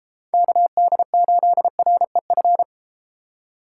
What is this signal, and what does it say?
Morse code, heard as a keyed tone, about 32 wpm, 700 Hz. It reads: KD8REF